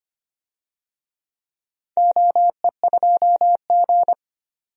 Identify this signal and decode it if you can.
Morse code, heard as a keyed tone, about 25 wpm, 700 Hz. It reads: OE2G
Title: DXCC-100 Call Signs - 4 Characters - 25wpm